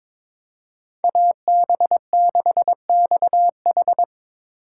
Morse code, heard as a keyed tone, about 22 wpm, 700 Hz. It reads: AB6XH
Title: USA Call Signs - 5 Characters - 6x Character Spacing - 22wpm